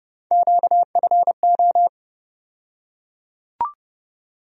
Morse code, heard as a keyed tone, about 31 wpm, 700 Hz. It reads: QFO
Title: USA Call Sign Postfixes - 3 Letters 30wpm